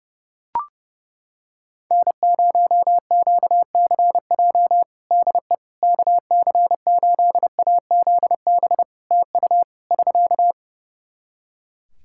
Morse code, heard as a keyed tone, about 32 wpm, 700 Hz. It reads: N0QCJ DE KC8AZB TU <SK>
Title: Practice QSO - Exchange 4 - Ending - 30wpm